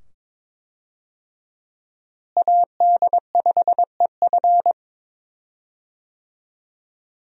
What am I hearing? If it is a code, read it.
Morse code, heard as a keyed tone, about 22 wpm, 700 Hz. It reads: AD5EF
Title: USA Call Signs - 5 Characters - Repeated 2x - 22wpm